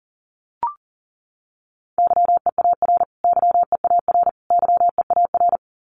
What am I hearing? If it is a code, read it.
Morse code, heard as a keyed tone, about 37 wpm, 700 Hz. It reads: YEAR YEAR YEAR
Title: Top 500 Words Repeated 40wpm